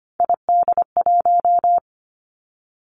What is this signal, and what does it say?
Morse code, heard as a keyed tone, about 25 wpm, 700 Hz. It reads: ID1
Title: Three Challenging Letter-Number Combinations 25wpm